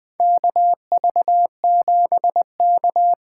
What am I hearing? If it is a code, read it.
Morse code, heard as a keyed tone, about 21 wpm, 700 Hz. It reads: KV7K
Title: USA Call Signs - 4 Characters - 4x Character Spacing - 20wpm